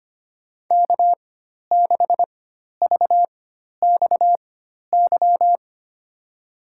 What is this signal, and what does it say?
Morse code, heard as a keyed tone, about 25 wpm, spaced as for 12 wpm, 700 Hz. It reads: K6VXY
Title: USA Call Signs - 5 Characters - 4x Character Spacing - 25wpm